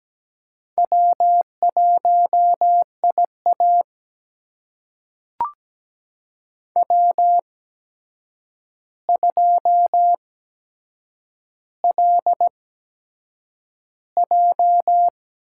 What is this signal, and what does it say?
Morse code, heard as a keyed tone, about 17 wpm, spaced as for 7 wpm, 700 Hz. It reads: W1IA W2LJ
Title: USA Call Signs - 4 Characters - 8x Character Spacing - 17wpm